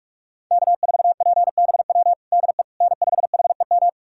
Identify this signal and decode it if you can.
Morse code, heard as a keyed tone, about 41 wpm, 700 Hz. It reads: K4PBW DE N5HEM
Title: USA Callsign DE Callsign - 5 Characters - 45wpm